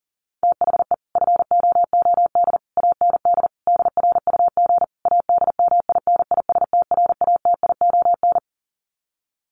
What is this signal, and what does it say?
Morse code, heard as a keyed tone, about 40 wpm, 700 Hz. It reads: THE FOOD AND DRUG ADMINISTRATION
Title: Sets of 5 Words 40wpm